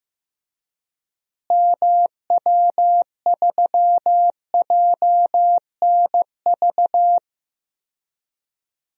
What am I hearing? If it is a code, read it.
Morse code, heard as a keyed tone, about 15 wpm, 700 Hz. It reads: MW3JNV
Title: DXCC-100 Call Signs - 6 Characters - Repeated 3x - 15wpm